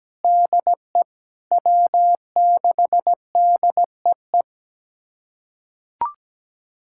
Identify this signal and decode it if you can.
Morse code, heard as a keyed tone, about 17 wpm, 700 Hz. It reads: DE W6DEE
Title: USA Callsign DE Callsign - 5 Characters - 17wpm